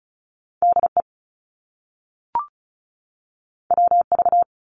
Morse code, heard as a keyed tone, about 36 wpm, 700 Hz. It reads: DE WV
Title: US States 35wpm